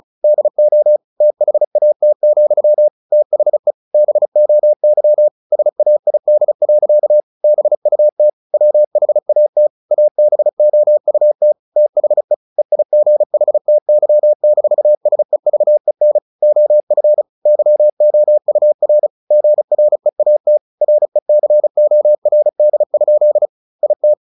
Morse code, heard as a keyed tone, about 34 wpm, 600 Hz. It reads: DO THAT, THE BOY SAID. BUT WHAT ABOUT THE EIGHTY-SEVEN OF YOUR GREAT RECORD? IT
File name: the old man and the sea / the old man and the sea 1 35wpm